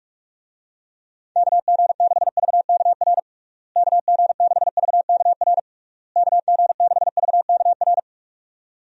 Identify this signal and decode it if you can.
Morse code, heard as a keyed tone, about 46 wpm, 700 Hz. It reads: KG6VKR KG6VKR KG6VKR
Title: USA Call Signs - 6 Characters - Repeated 3x - 45wpm